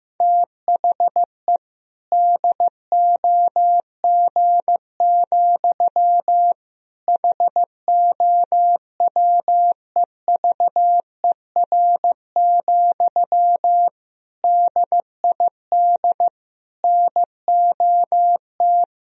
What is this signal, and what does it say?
Morse code, heard as a keyed tone, about 15 wpm, 700 Hz. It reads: THE DOG, HOWEVER, DID NOT
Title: Sentences from Top 1000 Words - Review Part 3 - 15wpm